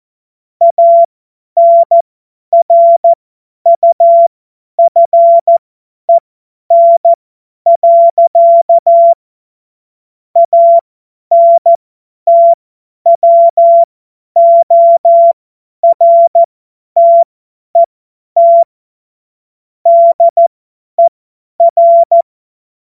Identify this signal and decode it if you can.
Morse code, heard as a keyed tone, about 14 wpm, spaced as for 10 wpm, 680 Hz. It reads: ANRUFEN. ANTWORTET DER
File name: Witz-23-September-2025-50-F70